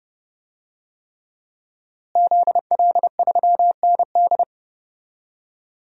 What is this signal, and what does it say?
Morse code, heard as a keyed tone, about 30 wpm, 700 Hz. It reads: ZL3ND
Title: DXCC-100 Call Signs - 5 Characters - Repeated 3x - 30wpm